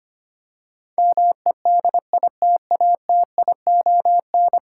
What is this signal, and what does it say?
Morse code, heard as a keyed tone, about 25 wpm, 700 Hz. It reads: MEDITATION
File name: 10 Letter Words - First 3 Letters Given - 25wpm